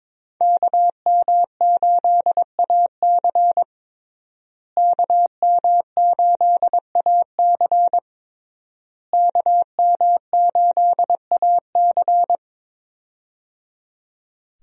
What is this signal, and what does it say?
Morse code, heard as a keyed tone, about 22 wpm, 700 Hz. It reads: KM8AC KM8AC KM8AC
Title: USA Call Signs - 5 Characters - Repeated 3x - 22wpm